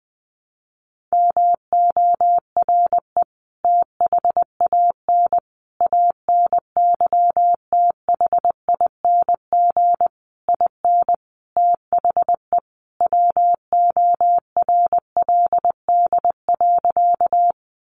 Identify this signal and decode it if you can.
Morse code, heard as a keyed tone, about 20 wpm, 700 Hz. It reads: MORE THAN ANYTHING IN THE WORLD.
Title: Sentences from Top 1000 Words 20wpm